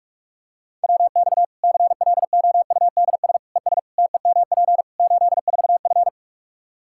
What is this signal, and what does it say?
Morse code, heard as a keyed tone, about 46 wpm, 700 Hz. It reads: WX CLOUDS ES TEMP 84F
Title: Practice QSO - Exchange 2 - Info - 45wpm